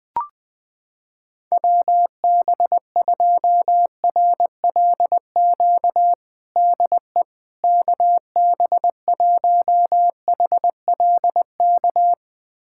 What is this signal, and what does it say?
Morse code, heard as a keyed tone, about 20 wpm, 700 Hz. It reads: WB2RLQ DE KB1HLK
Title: USA Callsign DE Callsign - 6 Characters - 20wpm